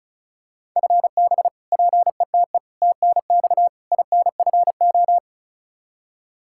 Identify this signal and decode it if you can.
Morse code, heard as a keyed tone, about 36 wpm, 700 Hz. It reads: FB PETE TNX INFO
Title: Practice QSO - Exchange 3 - More Info - 35wpm